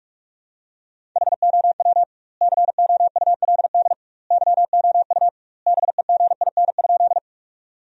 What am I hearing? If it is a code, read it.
Morse code, heard as a keyed tone, about 46 wpm, 700 Hz. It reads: HOW COULD YOU BEGIN?
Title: 2x Word-Spacing - Sentences from Top 300 Words - 45wpm